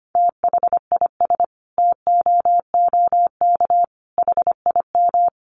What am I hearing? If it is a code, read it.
Morse code, heard as a keyed tone, about 25 wpm, 700 Hz. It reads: THIS TOOK HIM
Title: Sentences from Top 300 Words 25wpm